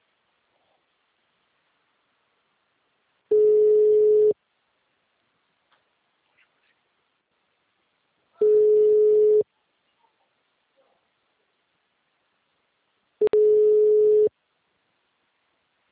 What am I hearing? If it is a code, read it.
Ringback tone (Europe, 425 Hz).